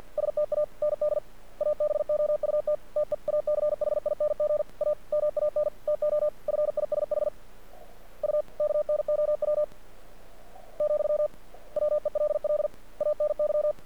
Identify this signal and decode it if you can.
Morse code, heard as a keyed tone, about 50 wpm, 600 Hz. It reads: STAND ABOUT TEACHING A MAN TO FISH U KNOW , WELL ANY
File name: archived / CRSnet